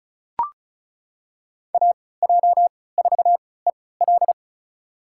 Morse code, heard as a keyed tone, about 36 wpm, spaced as for 20 wpm, 700 Hz. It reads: AJ4EL
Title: USA Call Signs - 3x Character Spacing - 35wpm